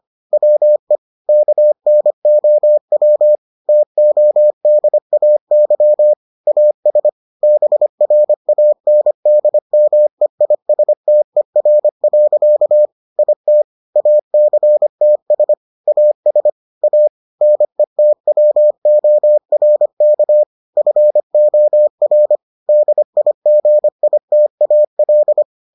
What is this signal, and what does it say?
Morse code, heard as a keyed tone, about 25 wpm, 600 Hz. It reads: WE KNOW TODAY AS BRANDMEISTER. IT ACTS AS A NETWORK FOR DIGITAL